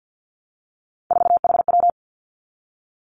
Morse code, heard as a keyed tone, about 50 wpm, 700 Hz. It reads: 4HW
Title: Three Challenging Letter-Number Combinations 50wpm